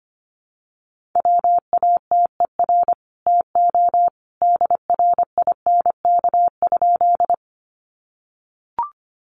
Morse code, heard as a keyed tone, about 25 wpm, 700 Hz. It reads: WATER TO DRINK?
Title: Sentences from Top 1500 Words 25wpm